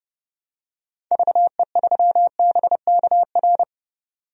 Morse code, heard as a keyed tone, about 30 wpm, 700 Hz. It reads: VE3BKR
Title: DXCC-100 Call Signs - 6 Characters - 30wpm